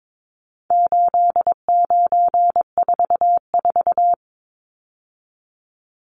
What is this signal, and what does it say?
Morse code, heard as a keyed tone, about 22 wpm, 700 Hz. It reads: 8944
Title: Numbers - 4-Digit 22wpm